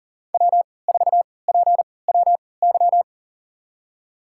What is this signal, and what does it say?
Morse code, heard as a keyed tone, about 40 wpm, spaced as for 23 wpm, 700 Hz. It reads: W4PWY
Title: USA Call Signs - 5 Characters - 3x Character Spacing - 40wpm